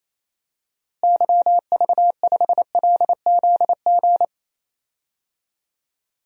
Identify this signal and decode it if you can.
Morse code, heard as a keyed tone, about 28 wpm, 700 Hz. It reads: YV5LZG
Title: DXCC-100 Call Signs - 6 Characters - Repeated 3x - 28wpm